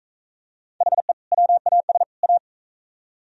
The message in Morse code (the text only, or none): HE WAS A